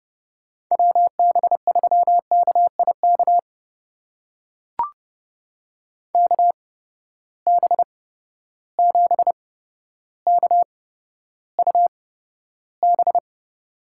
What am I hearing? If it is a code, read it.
Morse code, heard as a keyed tone, about 30 wpm, spaced as for 13 wpm, 700 Hz. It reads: WB3KIK KB7KUB
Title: USA Call Signs - 6 Characters - 8x Character Spacing - 30wpm